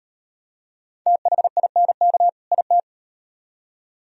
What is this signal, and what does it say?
Morse code, heard as a keyed tone, about 38 wpm, 700 Hz. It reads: THINK IT